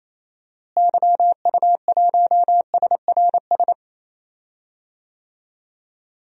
Morse code, heard as a keyed tone, about 27 wpm, 700 Hz. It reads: YU1SRS
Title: DXCC-100 Call Signs - 6 Characters - Repeated 2x - 28wpm